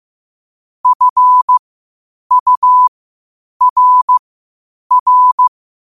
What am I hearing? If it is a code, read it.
Morse code, heard as a keyed tone, about 15 wpm, spaced as for 8 wpm, 1000 Hz. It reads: FURR
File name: qotdfiles / 8wpm